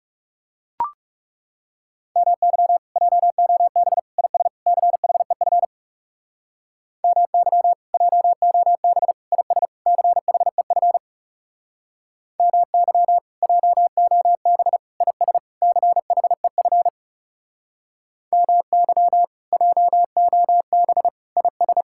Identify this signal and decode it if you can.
Morse code, heard as a keyed tone, about 38 wpm, 700 Hz. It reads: MY JOB IS CHEF MY JOB IS CHEF MY JOB IS CHEF MY JOB IS